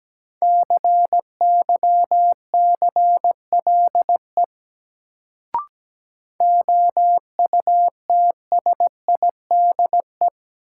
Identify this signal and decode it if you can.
Morse code, heard as a keyed tone, about 17 wpm, 700 Hz. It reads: CYCLE OUTSIDE
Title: Sentences from Top 900 Words - New Word Drill - 17wpm